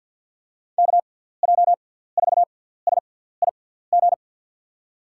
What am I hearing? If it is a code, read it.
Morse code, heard as a keyed tone, about 49 wpm, spaced as for 17 wpm, 700 Hz. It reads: KJ4SIG
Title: USA Call Signs - 6 Characters - 6x Character Spacing - 50wpm